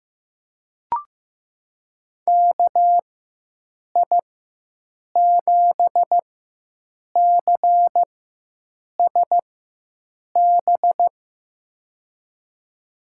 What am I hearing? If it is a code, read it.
Morse code, heard as a keyed tone, about 15 wpm, spaced as for 7 wpm, 700 Hz. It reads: KI7CSB
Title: USA Call Signs - 6 Characters - 4x Character Spacing - 15wpm